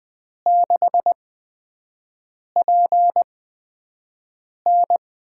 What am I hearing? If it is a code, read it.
Morse code, heard as a keyed tone, about 20 wpm, spaced as for 5 wpm, 700 Hz. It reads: 6PN
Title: USA Call Signs - 4 Characters - 8x Character Spacing - 20wpm